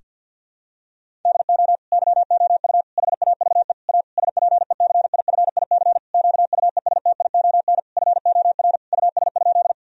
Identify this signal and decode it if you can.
Morse code, heard as a keyed tone, about 50 wpm, 700 Hz. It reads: DO YOU HAVE A SPECIFIC QUESTION FOR US?
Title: Sentences from Top 1000 Words - Review Part 3 - 50wpm